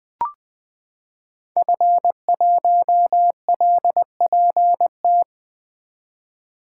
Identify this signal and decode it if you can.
Morse code, heard as a keyed tone, about 21 wpm, 700 Hz. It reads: F1LPT
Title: 2023 POTA Activator Call Signs - 20wpm